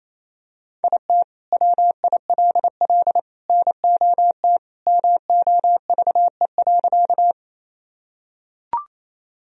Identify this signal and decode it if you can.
Morse code, heard as a keyed tone, about 28 wpm, 700 Hz. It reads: IT WILL NOT MOVE.
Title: Sentences from Top 1000 Words - Review Part 1 - 28wpm